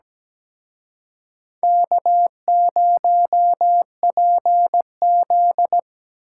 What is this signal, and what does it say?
Morse code, heard as a keyed tone, about 21 wpm, 700 Hz. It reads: K0PZ